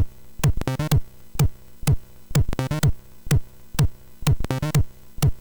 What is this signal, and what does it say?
There is no code.